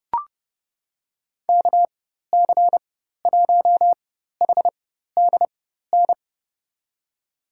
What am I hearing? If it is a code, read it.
Morse code, heard as a keyed tone, about 30 wpm, spaced as for 14 wpm, 700 Hz. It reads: KC1HDN
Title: USA Call Signs - 6 Characters - 4x Character Spacing - 30wpm